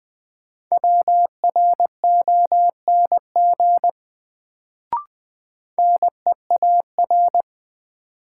Morse code, heard as a keyed tone, about 20 wpm, 700 Hz. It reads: WRONG NEAR
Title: Sets of 5 Words Encore - New Word Drill - 20wpm